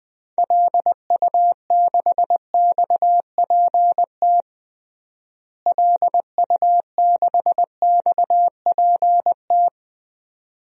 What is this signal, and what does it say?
Morse code, heard as a keyed tone, about 20 wpm, 700 Hz. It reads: LU6XPT LU6XPT
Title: DXCC-100 Call Signs - 6 Characters - Repeated 2x - 20wpm